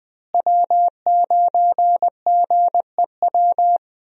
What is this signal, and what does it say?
Morse code, heard as a keyed tone, about 20 wpm, 700 Hz. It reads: W9GEW
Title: USA Call Signs - 5 Characters - Repeated 2x - 20wpm